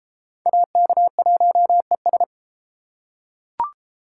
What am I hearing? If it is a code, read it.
Morse code, heard as a keyed tone, about 33 wpm, 700 Hz. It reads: AK1ES